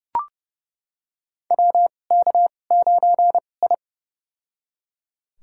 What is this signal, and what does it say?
Morse code, heard as a keyed tone, about 30 wpm, spaced as for 22 wpm, 700 Hz. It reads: WK9I
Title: USA Call Signs - 4 Characters - 2x Character Spacing - 30wpm